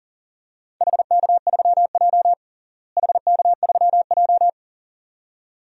Morse code, heard as a keyed tone, about 39 wpm, 690 Hz. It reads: HK3J HK3J
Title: DXCC-100 Call Signs - 4 Characters - Repeated 2x - 40wpm